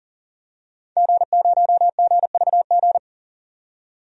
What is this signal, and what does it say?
Morse code, heard as a keyed tone, about 40 wpm, 700 Hz. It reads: G0GVG
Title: DXCC-100 Call Signs - 5 Characters - 40wpm